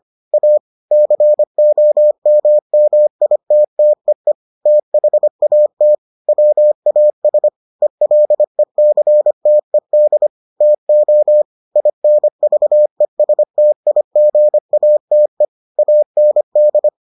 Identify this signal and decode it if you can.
Morse code, heard as a keyed tone, about 25 wpm, 600 Hz. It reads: A COMMITTEE THAT WAS ELECTED TO INVESTIGATE AND